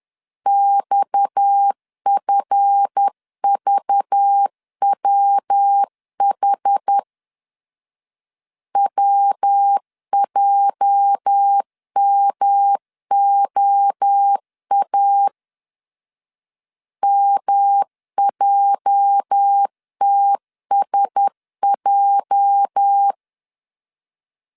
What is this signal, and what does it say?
Morse code, heard as a keyed tone, about 10 wpm, 800 Hz. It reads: XFVWH WJMOA MJTSJ
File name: LEZIONI / 8 YJ45 / 8 YJ 10wpm 2651